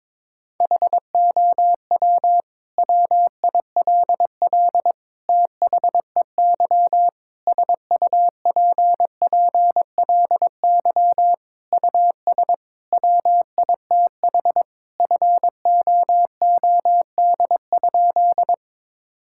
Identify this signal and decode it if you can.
Morse code, heard as a keyed tone, about 22 wpm, 700 Hz. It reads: HOW WILL THEY SUPPLY US WITH FOOD?
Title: Sentences from Top 1000 Words - Review Part 5 - 22wpm